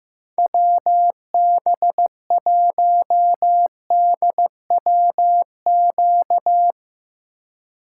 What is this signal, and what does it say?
Morse code, heard as a keyed tone, about 16 wpm, 700 Hz. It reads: WB1DWQ
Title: USA Call Signs - 6 Characters - 4x Character Spacing - 15wpm